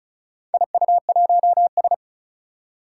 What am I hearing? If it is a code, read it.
Morse code, heard as a keyed tone, about 34 wpm, 700 Hz. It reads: IU1S
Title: Four Challenging Letter-Number Combinations 35wpm